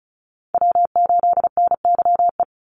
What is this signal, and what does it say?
Morse code, heard as a keyed tone, about 34 wpm, 700 Hz. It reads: W8NYE
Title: USA Call Signs 35wpm